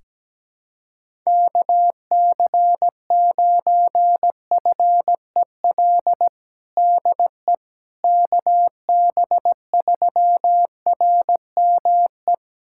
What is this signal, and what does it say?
Morse code, heard as a keyed tone, about 17 wpm, 700 Hz. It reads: KC9FEL DE KB3RME
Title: USA Callsign DE Callsign - 6 Characters - 17wpm